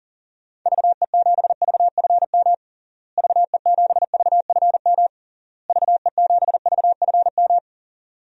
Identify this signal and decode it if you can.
Morse code, heard as a keyed tone, about 40 wpm, 700 Hz. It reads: VE7VFM VE7VFM VE7VFM
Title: DXCC-100 Call Signs - 6 Characters - Repeated 3x - 40wpm